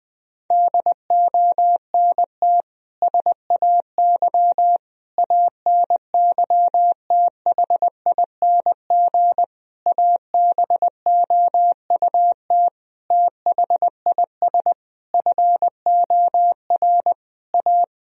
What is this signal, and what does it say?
Morse code, heard as a keyed tone, about 20 wpm, 700 Hz. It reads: DONT SAY ANYTHING ABOUT THIS FOR A
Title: Sentences from Top 1000 Words - Review Part 1 - 20wpm